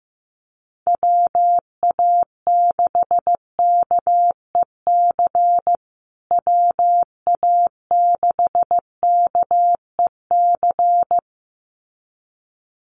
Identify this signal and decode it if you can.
Morse code, heard as a keyed tone, about 15 wpm, 700 Hz. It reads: WA6KEC WA6KEC
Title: USA Call Signs Repeated 2x 15wpm